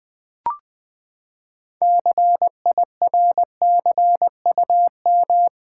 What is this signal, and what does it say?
Morse code, heard as a keyed tone, about 20 wpm, 700 Hz. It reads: CIRCUM
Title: Build A Word - 6 Letters At A Time - 12-Letter Words - 20wpm